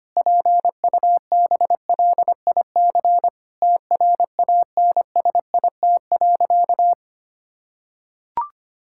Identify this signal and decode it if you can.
Morse code, heard as a keyed tone, about 25 wpm, 700 Hz. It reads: PUBLIC TRANSIT.
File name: practice / 25wpm / Difficult Sentences with Hints - 25wpm